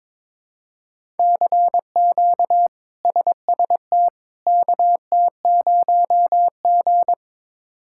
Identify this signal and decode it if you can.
Morse code, heard as a keyed tone, about 22 wpm, 700 Hz. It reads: CQ SST KT0G
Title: K1USN SST QSO Practice - 22wpm